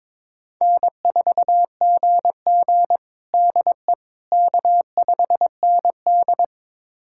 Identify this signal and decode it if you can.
Morse code, heard as a keyed tone, about 22 wpm, 700 Hz. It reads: N4GG DE K5ND
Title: USA Callsign DE Callsign - 4 Characters - 22wpm